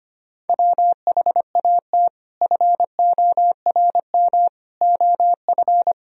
Morse code, heard as a keyed tone, about 25 wpm, 700 Hz. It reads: WHAT FORM OF
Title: Sentences from Top 1000 Words - Review Part 3 - 25wpm